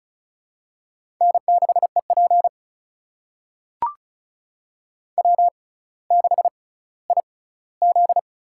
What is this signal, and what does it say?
Morse code, heard as a keyed tone, about 36 wpm, spaced as for 18 wpm, 700 Hz. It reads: N6EP W6IZ
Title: USA Call Signs - 4 Characters - 6x Character Spacing - 35wpm